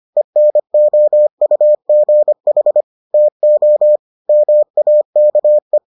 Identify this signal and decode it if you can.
Morse code, heard as a keyed tone, about 25 wpm, 600 Hz. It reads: ENOUGH TO MAKE